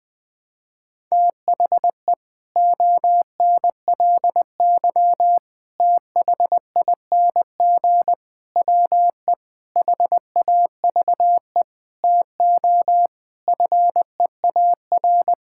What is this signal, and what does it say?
Morse code, heard as a keyed tone, about 20 wpm, 700 Hz. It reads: THE ONLY THING WE HAVE TO FEAR
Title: Sets of 8 Words - 20wpm